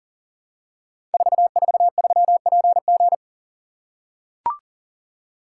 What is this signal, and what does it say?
Morse code, heard as a keyed tone, about 38 wpm, 690 Hz. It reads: 443PG
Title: Code Groups - 5 Letter:Numbers - 40wpm